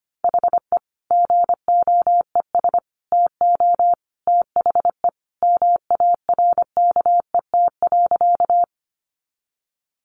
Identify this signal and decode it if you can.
Morse code, heard as a keyed tone, about 25 wpm, 700 Hz. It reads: HE GOES TO THE MARKET.
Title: Additional Easy Phrases 25wpm